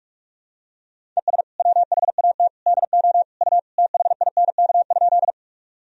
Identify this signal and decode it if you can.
Morse code, heard as a keyed tone, about 41 wpm, 700 Hz. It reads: ES WHAT DO U THINK?
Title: Sentences from Top 200 Words and Abbreviations 45wpm